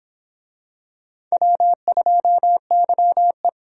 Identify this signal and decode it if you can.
Morse code, heard as a keyed tone, about 25 wpm, 700 Hz. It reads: W2YE